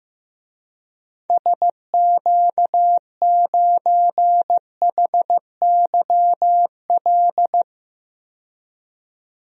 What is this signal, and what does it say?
Morse code, heard as a keyed tone, about 15 wpm, 690 Hz. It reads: SQ9HYL
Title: DXCC-100 Call Signs - 6 Characters - 15wpm